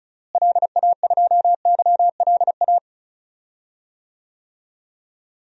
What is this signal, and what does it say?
Morse code, heard as a keyed tone, about 36 wpm, 700 Hz. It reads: LA2YLA
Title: DXCC-100 Call Signs - 6 Characters - Repeated 3x - 35wpm